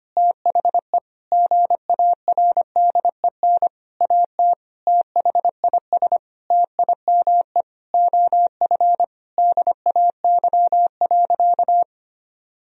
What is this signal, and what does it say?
Morse code, heard as a keyed tone, about 25 wpm, 700 Hz. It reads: THE GARDEN AT THIS TIME OF DAY.
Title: Sentences from Top 1000 Words - Review Part 4 - 25wpm